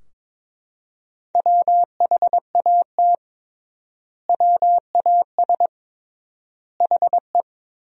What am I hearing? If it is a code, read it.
Morse code, heard as a keyed tone, about 22 wpm, 700 Hz. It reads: WHAT WAS HE
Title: 3x Word-Spacing - Sentences from Top 100 Words - 22wpm